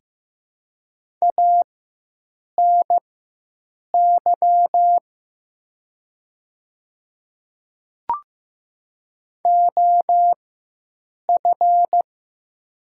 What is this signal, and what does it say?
Morse code, heard as a keyed tone, about 15 wpm, spaced as for 7 wpm, 700 Hz. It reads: ANY OF
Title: 4x Character Spacing - Top 200 Words - 15wpm